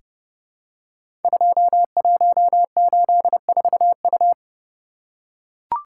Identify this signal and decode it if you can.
Morse code, heard as a keyed tone, about 30 wpm, 700 Hz. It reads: 2184U
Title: Code Groups - 5 Letter:Numbers - 30wpm